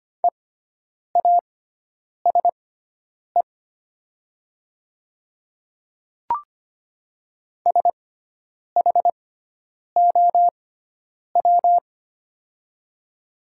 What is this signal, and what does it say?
Morse code, heard as a keyed tone, about 25 wpm, spaced as for 9 wpm, 700 Hz. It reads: EASE SHOW
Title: 6x Character Spacing - Top 500 Words - 25wpm